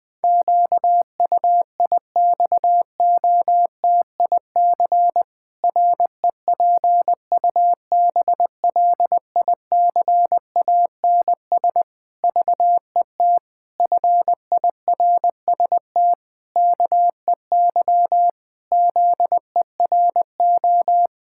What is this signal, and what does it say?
Morse code, heard as a keyed tone, about 20 wpm, 700 Hz. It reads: QUIXOTIC REPUBLICANS VET FIRST KEY ZERO
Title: Pangram Challenge v2 - 20wpm